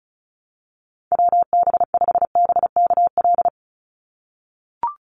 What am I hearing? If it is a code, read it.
Morse code, heard as a keyed tone, about 36 wpm, 700 Hz. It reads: WB5BKL
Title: NA SOTA Chaser Callsigns 35wpm